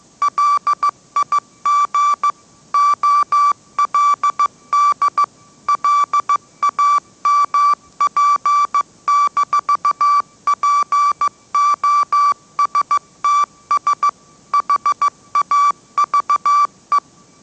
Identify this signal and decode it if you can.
Morse code, heard as a keyed tone, about 15 wpm, 1160 Hz. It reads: LIG OLD LAMP-POSTS HAVE
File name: mrx / mrx066 16wpm